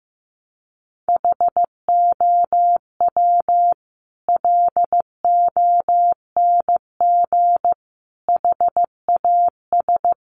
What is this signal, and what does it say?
Morse code, heard as a keyed tone, about 15 wpm, 700 Hz. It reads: HOW LONG HAS
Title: Sentences from Top 100 Words 15wpm